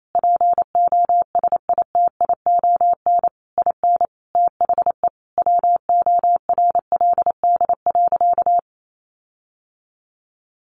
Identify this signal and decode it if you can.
Morse code, heard as a keyed tone, about 28 wpm, 700 Hz. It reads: POSITION IN THE WORLD.
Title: Sentences from Top 1300 Words 28wpm